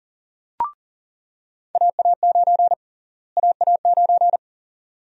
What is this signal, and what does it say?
Morse code, heard as a keyed tone, about 39 wpm, 700 Hz. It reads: AA9 AA9
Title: USA Call Sign Prefixes - Repeated 2x - 40wpm